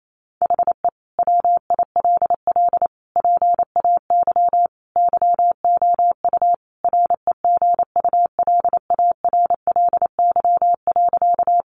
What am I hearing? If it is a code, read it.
Morse code, heard as a keyed tone, about 28 wpm, 700 Hz. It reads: HE WILL PAY YOU REGULARLY.
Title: Sentences from Top 1400 Words 28wpm